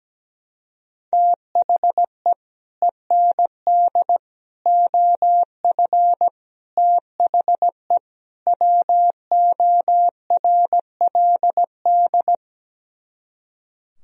Morse code, heard as a keyed tone, about 17 wpm, 700 Hz. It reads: THE END OF THE WORLD
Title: Speed-Racing - Sets of 5 Words 17wpm